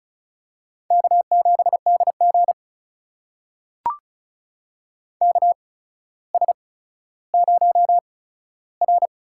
Morse code, heard as a keyed tone, about 34 wpm, spaced as for 15 wpm, 700 Hz. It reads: K7DG KS0R
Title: USA Call Signs - 4 Characters - 8x Character Spacing - 35wpm